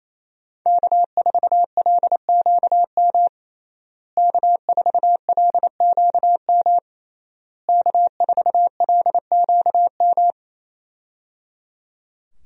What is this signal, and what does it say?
Morse code, heard as a keyed tone, about 27 wpm, 700 Hz. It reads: K4LQM K4LQM K4LQM
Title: USA Call Signs - 5 Characters - Repeated 3x - 28wpm